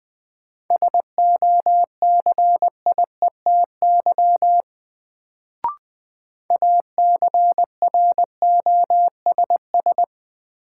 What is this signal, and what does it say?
Morse code, heard as a keyed tone, about 20 wpm, 700 Hz. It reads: SOCIETY ACROSS
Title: Sentences from Top 700 Words - New Word Drill - 20wpm